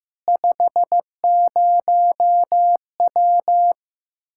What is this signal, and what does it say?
Morse code, heard as a keyed tone, about 15 wpm, 700 Hz. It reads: 50W
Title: Practice QSO - Exchange 2 - Info - 15wpm